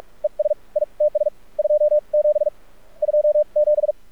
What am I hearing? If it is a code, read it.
Morse code, heard as a keyed tone, about 46 wpm, spaced as for 37 wpm, 600 Hz. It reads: ES ITS 27 27
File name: archived / CRSnet